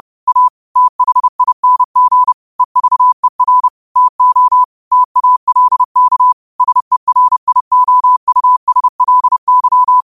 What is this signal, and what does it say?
Morse code, heard as a keyed tone, about 30 wpm, 1000 Hz. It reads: A THING EVER TO TALK SERIOUSLY